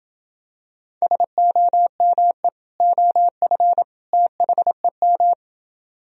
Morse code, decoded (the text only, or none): SOME OF THEM